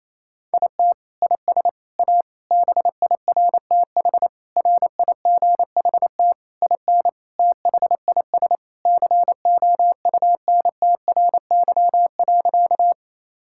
Morse code, heard as a keyed tone, about 28 wpm, 700 Hz. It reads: IT IS A BIRTH RIGHT IN THIS COUNTRY.
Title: Sentences from Top 1000 Words - Review Part 5 - 28wpm